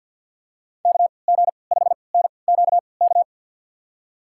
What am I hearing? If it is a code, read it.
Morse code, heard as a keyed tone, about 52 wpm, spaced as for 28 wpm, 700 Hz. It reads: KG5NQK
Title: USA Call Signs - 6 Characters - 3x Character Spacing - 50wpm